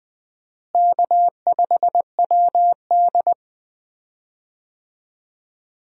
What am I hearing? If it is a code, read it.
Morse code, heard as a keyed tone, about 20 wpm, 700 Hz. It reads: K5WD
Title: USA Call Signs - 4 Characters - Repeated 3x - 20wpm